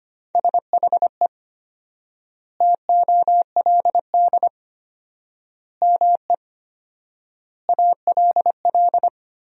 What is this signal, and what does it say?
Morse code, heard as a keyed tone, about 25 wpm, 700 Hz. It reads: SHE TOLD ME ALL
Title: 4x Word-Spacing - Sentences from Top 400 Words - 25wpm